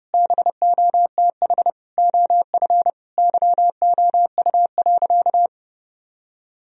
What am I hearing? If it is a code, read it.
Morse code, heard as a keyed tone, about 30 wpm, 700 Hz. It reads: BOTH OF YOU.
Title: Sentences from Top 1000 Words - Review Part 1 - 30wpm